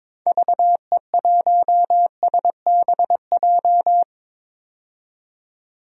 Morse code, heard as a keyed tone, about 22 wpm, 700 Hz. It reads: VE1SBJ
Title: DXCC-100 Call Signs - 6 Characters - 22wpm